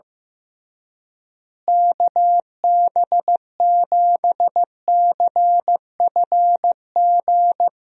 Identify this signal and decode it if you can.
Morse code, heard as a keyed tone, about 15 wpm, 700 Hz. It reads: KB7CFG
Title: USA Call Signs - 6 Characters - Repeated 3x - 15wpm